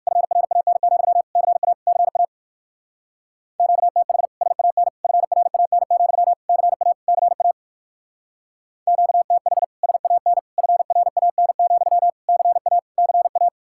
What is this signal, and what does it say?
Morse code, heard as a keyed tone, about 63 wpm, 700 Hz. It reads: FRAN, CA CA QTH SAN FRAN, CA CA QTH SAN FRAN, CA CA